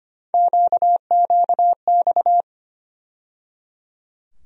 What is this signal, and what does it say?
Morse code, heard as a keyed tone, about 25 wpm, 700 Hz. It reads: QQX
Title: USA Call Sign Postfixes - 3 Letters 25wpm